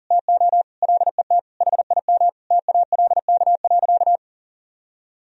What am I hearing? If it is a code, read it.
Morse code, heard as a keyed tone, about 40 wpm, 700 Hz. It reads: TO LET HIM TALK.
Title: Sentences from Top 1000 Words - Review Part 4 - 40wpm